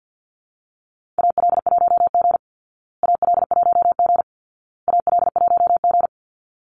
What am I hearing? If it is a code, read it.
Morse code, heard as a keyed tone, about 50 wpm, 700 Hz. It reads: AL1G AL1G AL1G
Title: Call Signs Repeated 2x - 50wpm